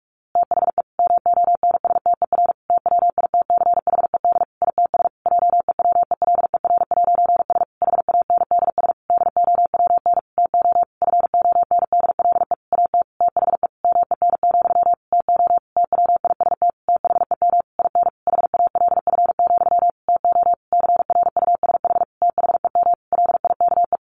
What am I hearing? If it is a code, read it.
Morse code, heard as a keyed tone, about 44 wpm, 700 Hz. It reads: THE MONSTER TWITCHED ITS JEWELER'S HANDS DOWN TO FONDLE AT THE MEN, TO TWIST THEM IN HALF, TO CRUSH THEM LIKE
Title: A Sound Of Thunder 45wpm